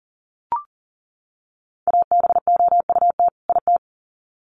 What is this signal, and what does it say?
Morse code, heard as a keyed tone, about 40 wpm, 700 Hz. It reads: ABOUT IT
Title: Sets of 2 Words 40wpm